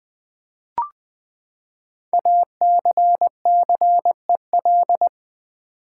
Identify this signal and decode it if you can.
Morse code, heard as a keyed tone, about 20 wpm, 700 Hz. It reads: ACCEL
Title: Build A Word - 5 Letters At A Time - 10-Letter Words - 20wpm